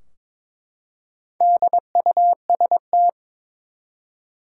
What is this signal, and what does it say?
Morse code, heard as a keyed tone, about 22 wpm, 700 Hz. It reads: DUST
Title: Four Letter Words - 22wpm